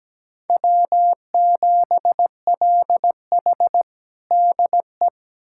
Morse code, heard as a keyed tone, about 17 wpm, 700 Hz. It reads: W7LH DE
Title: USA Callsign DE Callsign - 4 Characters - 17wpm